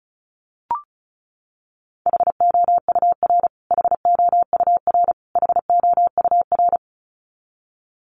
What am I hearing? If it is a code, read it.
Morse code, heard as a keyed tone, about 34 wpm, 700 Hz. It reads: HOUR HOUR HOUR
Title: Top 500 Words Repeated 35wpm